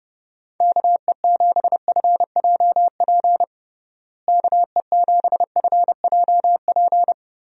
Morse code, heard as a keyed tone, about 30 wpm, 700 Hz. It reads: KE7FJP KE7FJP
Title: USA Call Signs - 6 Characters - Repeated 3x - 30wpm